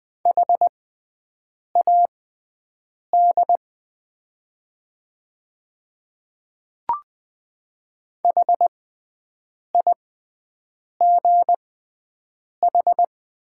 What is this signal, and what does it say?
Morse code, heard as a keyed tone, about 20 wpm, spaced as for 7 wpm, 690 Hz. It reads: HAD HIGH
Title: 6x Character Spacing - Top 300 Words - 20wpm